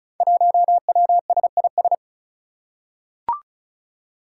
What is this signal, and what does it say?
Morse code, heard as a keyed tone, about 36 wpm, 700 Hz. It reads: 1WSIS
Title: Five Challenging Letter-Number Combinations 35wpm